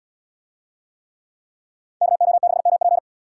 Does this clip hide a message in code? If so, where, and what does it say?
Yes: Morse code from 2.0 to 3.0 s, heard as a keyed tone, about 70 wpm, 700 Hz. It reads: KO6MP